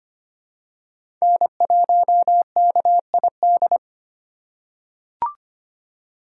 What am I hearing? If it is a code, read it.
Morse code, heard as a keyed tone, about 25 wpm, 700 Hz. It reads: N1KID
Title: USA Call Signs - 8x Character Spacing - 25wpm